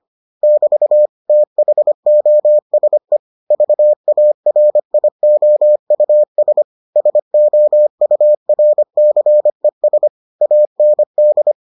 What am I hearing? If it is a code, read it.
Morse code, heard as a keyed tone, about 25 wpm, 600 Hz. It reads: <BT> THOSE VARIOUS SOURCES AND